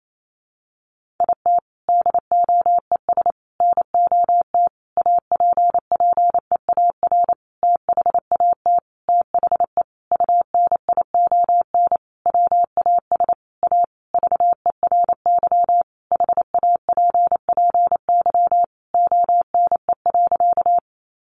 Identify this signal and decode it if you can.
Morse code, heard as a keyed tone, about 28 wpm, 700 Hz. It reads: IT DOES NOT APPEAR THAT THE UNION WAS A VERY HAPPY ONE.
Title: Sentences from Top 1400 Words 28wpm